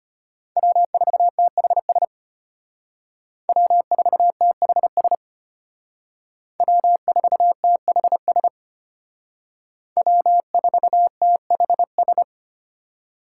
Morse code, heard as a keyed tone, about 32 wpm, 700 Hz. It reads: W4THS W4THS W4THS W4THS